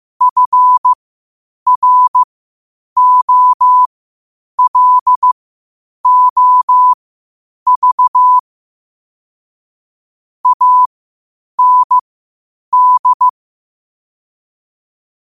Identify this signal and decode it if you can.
Morse code, heard as a keyed tone, about 15 wpm, spaced as for 8 wpm, 1000 Hz. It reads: FROLOV AND
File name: qotdfiles / 8wpm